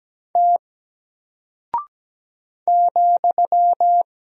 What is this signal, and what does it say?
Morse code, heard as a keyed tone, about 17 wpm, 700 Hz. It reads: T ,
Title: Single Character (Letter:Number:Punctuation:Prosign) Alt 17wpm